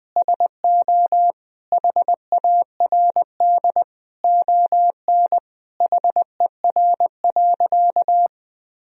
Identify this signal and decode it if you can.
Morse code, heard as a keyed tone, about 21 wpm, 700 Hz. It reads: SO HARD ON HER.
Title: Sentences from Top 1000 Words - Review Part 1 - 20wpm